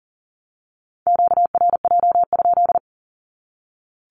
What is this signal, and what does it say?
Morse code, heard as a keyed tone, about 40 wpm, 700 Hz. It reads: QRJ?
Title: Q-Codes 40wpm